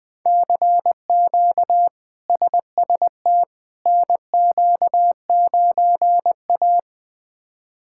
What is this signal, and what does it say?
Morse code, heard as a keyed tone, about 20 wpm, 700 Hz. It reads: CQ SST NQ9A
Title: K1USN SST QSO Practice - 20wpm